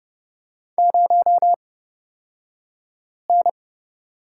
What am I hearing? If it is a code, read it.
Morse code, heard as a keyed tone, about 30 wpm, spaced as for 5 wpm, 700 Hz. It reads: 0N
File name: Lesson 37d - Letter x - Two Characters 30wpm